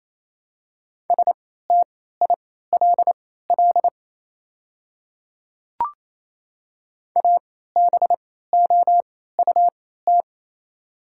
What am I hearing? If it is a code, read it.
Morse code, heard as a keyed tone, about 28 wpm, spaced as for 16 wpm, 700 Hz. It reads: STILL ABOUT
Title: 3x Character Spacing - Top 300 Words - 28wpm